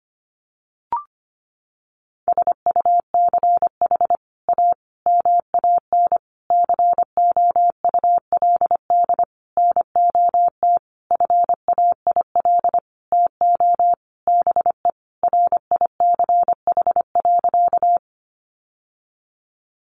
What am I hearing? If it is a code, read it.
Morse code, heard as a keyed tone, about 25 wpm, 700 Hz. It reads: SUCH A MAN COULD NOT FAIL TO BE RICH.
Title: Sentences from Top 1100 Words 25wpm